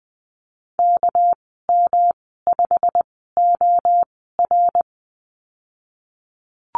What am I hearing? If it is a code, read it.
Morse code, heard as a keyed tone, about 20 wpm, spaced as for 15 wpm, 700 Hz. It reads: KM5OR
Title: 207 Lesson 33g - Letter j - Call Signs 20wpm